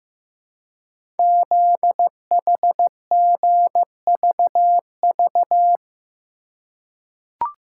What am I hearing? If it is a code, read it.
Morse code, heard as a keyed tone, about 15 wpm, 700 Hz. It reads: ZHGVV
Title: Code Groups - 5 Letters - 15wpm